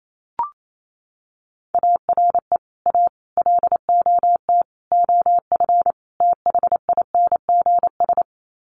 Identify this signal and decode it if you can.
Morse code, heard as a keyed tone, about 28 wpm, 700 Hz. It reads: ARE A LOT OF THINGS
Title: Sets of 5 Words Encore2 28wpm